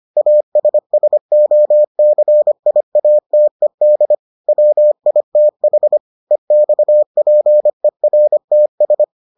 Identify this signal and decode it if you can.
Morse code, heard as a keyed tone, about 25 wpm, 600 Hz. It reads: ASSOCIATED WITH EXPERTS